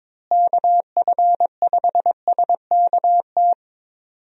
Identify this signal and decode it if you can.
Morse code, heard as a keyed tone, about 22 wpm, 700 Hz. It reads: KF5SKT
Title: USA Call Signs - 6 Characters - Repeated 2x - 22wpm